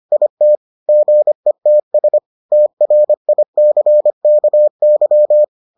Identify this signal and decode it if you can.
Morse code, heard as a keyed tone, about 25 wpm, 600 Hz. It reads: IT GETS TRICKY